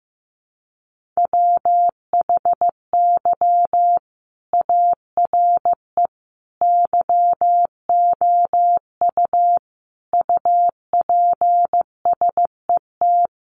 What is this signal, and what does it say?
Morse code, heard as a keyed tone, about 15 wpm, 700 Hz. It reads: WHY ARE YOU UPSET
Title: Sentences from Top 800 Words 15wpm